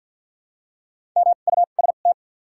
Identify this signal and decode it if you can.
Morse code, heard as a keyed tone, about 54 wpm, spaced as for 37 wpm, 700 Hz. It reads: MUST
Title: 2x Character Spacing - Top 200 Words - 50wpm